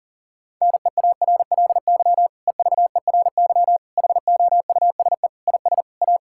Morse code, heard as a keyed tone, about 40 wpm, 700 Hz. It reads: NEARLY EVERY HOUSE IS A
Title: Sentences from Top 1600 Words 40wpm